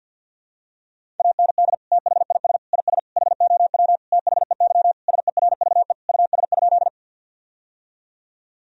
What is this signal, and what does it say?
Morse code, heard as a keyed tone, about 52 wpm, 700 Hz. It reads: AND THIS IS HOW THEY SERVE US?
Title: Sentences from Top 1000 Words - Review Part 2 - 50wpm